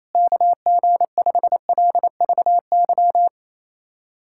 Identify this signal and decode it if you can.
Morse code, heard as a keyed tone, about 27 wpm, 700 Hz. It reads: KG5LVY
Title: USA Callsign DE Callsign - 6 Characters - 28wpm